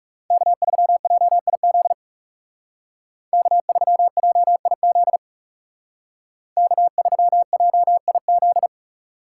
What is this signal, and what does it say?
Morse code, heard as a keyed tone, about 39 wpm, 700 Hz. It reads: K3JIZ K3JIZ K3JIZ